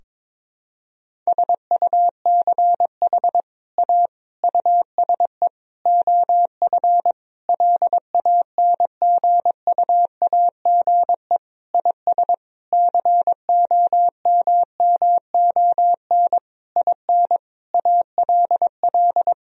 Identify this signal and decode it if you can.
Morse code, heard as a keyed tone, about 22 wpm, 700 Hz. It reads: SUCH A USE OF LANGUAGE IS COMMON IN ALL
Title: Sentences from Top 1000 Words - Review Part 3 - 22wpm